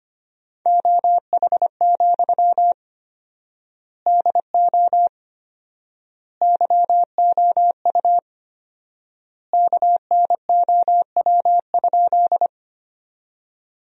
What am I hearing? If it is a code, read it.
Morse code, heard as a keyed tone, about 25 wpm, 700 Hz. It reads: OH, DO YOU KNOW?
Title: 4x Word-Spacing - Sentences from Top 500 Words - 25wpm